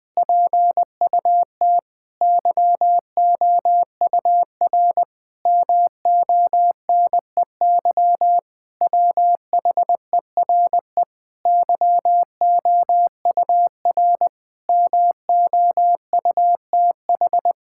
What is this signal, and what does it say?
Morse code, heard as a keyed tone, about 20 wpm, 700 Hz. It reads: PUT YOUR MONEY WHERE YOUR MOUTH
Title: Sentences with 7-Word Phrases - Spoken Beginning - 20wpm